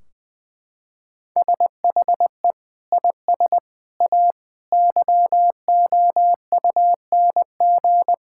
Morse code, heard as a keyed tone, about 20 wpm, 700 Hz. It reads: SHE IS A YOUNG
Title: Sentences from Top 1600 Words 20wpm